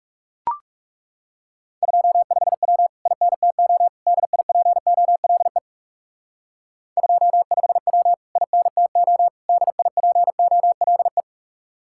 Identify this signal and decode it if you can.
Morse code, heard as a keyed tone, about 37 wpm, 700 Hz. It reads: 25W INTO DIPOLE 25W INTO DIPOLE
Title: Speed-Racing Common QSO Phrases - 30wpm